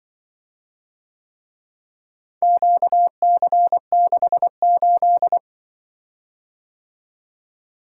Morse code, heard as a keyed tone, about 25 wpm, 700 Hz. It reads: QC68
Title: Speed-Racing - Four Letter-Number Combinations - 20wpm